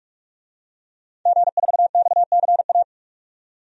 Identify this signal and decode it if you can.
Morse code, heard as a keyed tone, about 44 wpm, 690 Hz. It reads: G4XCA
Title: DXCC-100 Call Signs - 5 Characters - 45wpm